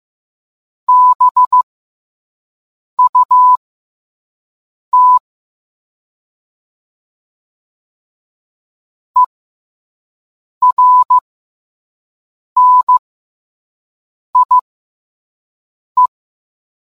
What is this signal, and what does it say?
Morse code, heard as a keyed tone, about 15 wpm, spaced as for 5 wpm, 1000 Hz. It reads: BUT ERNIE